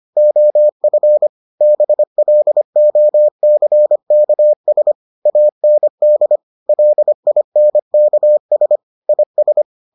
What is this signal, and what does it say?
Morse code, heard as a keyed tone, about 25 wpm, 600 Hz. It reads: OF BLOCKS AND LINKS IS